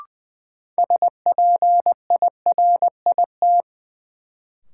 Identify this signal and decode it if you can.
Morse code, heard as a keyed tone, about 21 wpm, 700 Hz. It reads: SPIRIT
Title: Sentences from Top 1400 Words - New Word Drill Alt - 20wpm